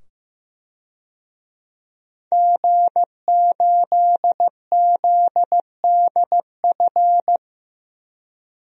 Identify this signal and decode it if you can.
Morse code, heard as a keyed tone, about 15 wpm, 700 Hz. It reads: G8ZDF
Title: DXCC-100 Call Signs - 5 Characters - 15wpm